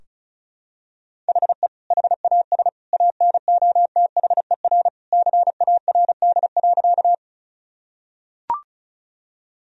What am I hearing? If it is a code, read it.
Morse code, heard as a keyed tone, about 36 wpm, 700 Hz. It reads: HE HAS ANOTHER CARD.
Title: Sentences from Top 1000 Words - Review Part 4 - 35wpm